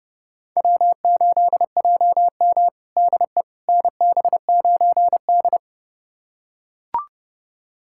Morse code, heard as a keyed tone, about 30 wpm, 700 Hz. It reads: W8JM DE NB9D
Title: USA Callsign DE Callsign - 4 Characters - 30wpm